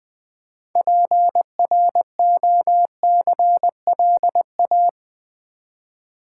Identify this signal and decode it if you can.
Morse code, heard as a keyed tone, about 20 wpm, 700 Hz. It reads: PROCLA
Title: Build A Word - 3 Letters At A Time - 8-Letter Words - 20wpm